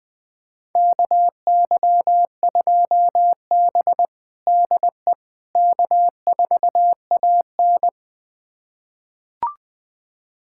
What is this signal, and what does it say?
Morse code, heard as a keyed tone, about 20 wpm, 700 Hz. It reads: KY2B DE K4AN
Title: USA Callsign DE Callsign - 4 Characters - 20wpm